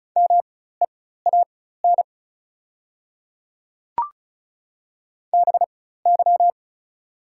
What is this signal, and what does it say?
Morse code, heard as a keyed tone, about 36 wpm, spaced as for 16 wpm, 690 Hz. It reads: MEAN BY
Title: 4x Character Spacing - Top 200 Words - 35wpm